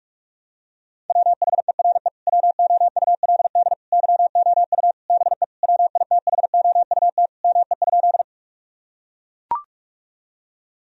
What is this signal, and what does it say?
Morse code, heard as a keyed tone, about 44 wpm, 700 Hz. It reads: WHERE WOULD YOU BE WITHOUT ME?
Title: Sentences from Top 1000 Words - Review Part 3 - 45wpm